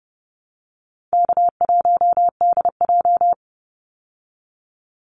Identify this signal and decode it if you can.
Morse code, heard as a keyed tone, about 28 wpm, 700 Hz. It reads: K1DJ
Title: Call Signs Repeated 3x - 30wpm